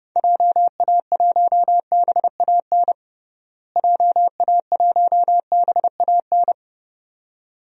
Morse code, heard as a keyed tone, about 30 wpm, 700 Hz. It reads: JA1BAN JA1BAN
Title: DXCC-100 Call Signs - 6 Characters - Repeated 2x - 30wpm